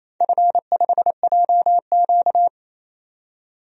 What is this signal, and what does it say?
Morse code, heard as a keyed tone, about 28 wpm, 700 Hz. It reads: F5JQ
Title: DXCC-100 Call Signs - 4 Characters - 28wpm